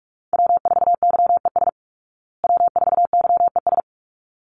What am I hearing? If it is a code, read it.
Morse code, heard as a keyed tone, about 44 wpm, 700 Hz. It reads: W4YES W4YES
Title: Call Signs Repeated 3x - 45wpm